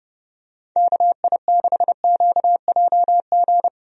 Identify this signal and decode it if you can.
Morse code, heard as a keyed tone, about 30 wpm, 700 Hz. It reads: KI6QJG
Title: USA Call Signs - 6 Characters - Repeated 2x - 30wpm